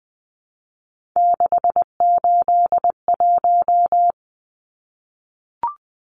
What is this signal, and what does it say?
Morse code, heard as a keyed tone, about 20 wpm, 700 Hz. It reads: 681
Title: Numbers - 3-Digit 20wpm